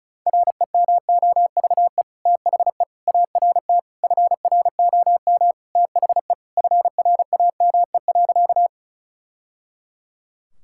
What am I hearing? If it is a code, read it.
Morse code, heard as a keyed tone, about 34 wpm, 690 Hz. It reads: REMOVE THE ART FROM THE FRAME.
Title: Sentences from Top 1000 Words - Review Part 6 - 35wpm